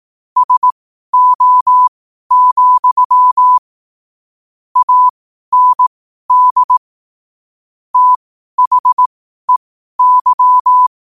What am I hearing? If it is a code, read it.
Morse code, heard as a keyed tone, about 18 wpm, spaced as for 12 wpm, 1000 Hz. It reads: SO, AND THEY
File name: qotdfiles / 12wpm